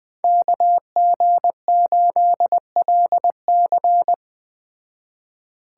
Morse code, heard as a keyed tone, about 20 wpm, 700 Hz. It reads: KG8LC
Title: Speed Racing - USA 2x2 Call Signs 17wpm